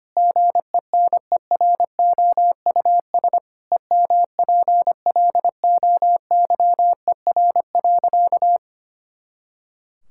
Morse code, heard as a keyed tone, about 25 wpm, 700 Hz. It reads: GENEROUS EMPLOYER.
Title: Pangram Challenge v2 - 25wpm